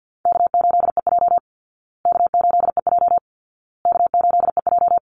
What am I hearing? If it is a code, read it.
Morse code, heard as a keyed tone, about 50 wpm, 700 Hz. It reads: K8EJ K8EJ K8EJ
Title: Easy Call Signs Repeated 3x - 50wpm